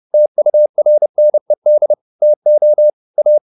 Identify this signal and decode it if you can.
Morse code, heard as a keyed tone, about 30 wpm, 600 Hz. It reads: TURNED TO A
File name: the old man and the sea / the old man and the sea 1 30wpm